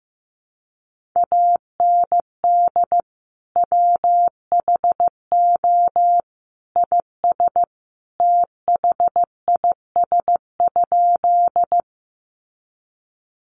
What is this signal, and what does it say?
Morse code, heard as a keyed tone, about 15 wpm, 700 Hz. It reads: AND WHO IS THIS?
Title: Sentences from Top 100 Words 15wpm